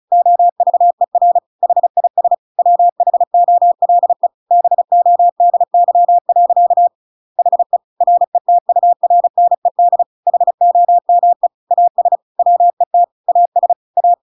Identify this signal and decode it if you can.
Morse code, heard as a keyed tone, about 34 wpm, 700 Hz. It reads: OVER HIS WHOLE BODY. HE RETURNED HOME AS WET AS A